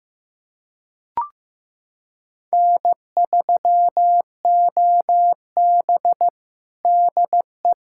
Morse code, heard as a keyed tone, about 15 wpm, 700 Hz. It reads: N3OB DE
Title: USA Callsign DE Callsign - 4 Characters - 15wpm